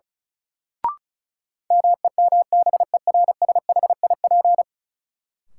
Morse code, heard as a keyed tone, about 36 wpm, 700 Hz. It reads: MEMBERSHIP
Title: Sentences from Top 1400 Words - New Word Drill - 35wpm